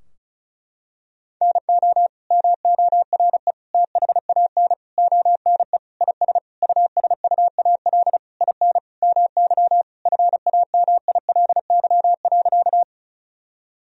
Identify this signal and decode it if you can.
Morse code, heard as a keyed tone, about 34 wpm, 700 Hz. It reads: NO MORE THAN ONE IS USUAL IN MY FAMILY.
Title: Sentences from Top 1000 Words - Review Part 2 - 35wpm